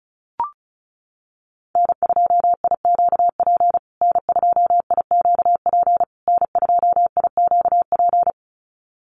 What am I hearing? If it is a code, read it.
Morse code, heard as a keyed tone, about 34 wpm, 700 Hz. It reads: N2IQP N2IQP N2IQP
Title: USA Call Signs Repeated 3x 35wpm